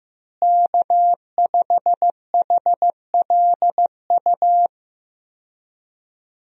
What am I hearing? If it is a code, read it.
Morse code, heard as a keyed tone, about 15 wpm, 700 Hz. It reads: K5HLU
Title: USA Call Signs - 5 Characters - Repeated 2x - 15wpm